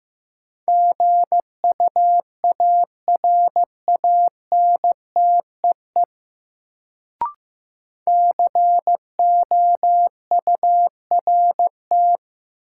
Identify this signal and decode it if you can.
Morse code, heard as a keyed tone, about 15 wpm, 700 Hz. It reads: GUARANTEE COURT
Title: Sentences from Top 1600 Words - New Word Drill - 15wpm